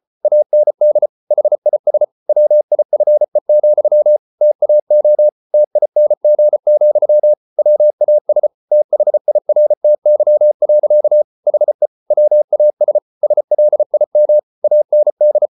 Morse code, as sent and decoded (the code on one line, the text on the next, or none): .- -. -.. / .... .. ... / .-- .. ..-. --..-- / - .- --- / - .. -. --. --..-- / .-- .- ... / - .... .. .-. - -.-- .-.-.- / .... / .-- .- ... / ... .-.. .. -- / .- -. -..
AND HIS WIFE, TAO TING, WAS THIRTY. HE WAS SLIM AND